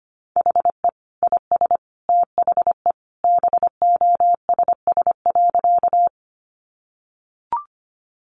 Morse code, heard as a keyed tone, about 25 wpm, 700 Hz. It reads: HE IS THE BOSS.
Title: Sentences from Top 700 Words 25wpm